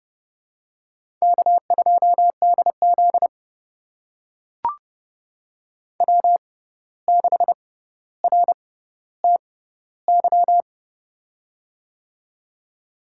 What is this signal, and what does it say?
Morse code, heard as a keyed tone, about 30 wpm, spaced as for 14 wpm, 700 Hz. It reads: K2DZ W6RTY
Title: USA Call Signs - 6x Character Spacing - 30wpm